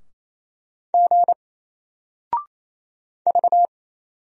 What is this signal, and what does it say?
Morse code, heard as a keyed tone, about 27 wpm, 700 Hz. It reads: G V
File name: Single-Letter-Number-28wpm